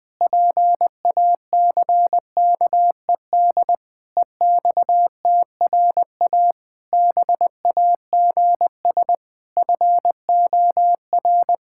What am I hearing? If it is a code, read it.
Morse code, heard as a keyed tone, about 20 wpm, 700 Hz. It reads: PACKED EXTRA BAGS FOR
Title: Pangram Challenge v2 - 20wpm